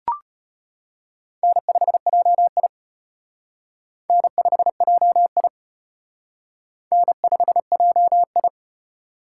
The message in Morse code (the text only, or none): N5JI N5JI N5JI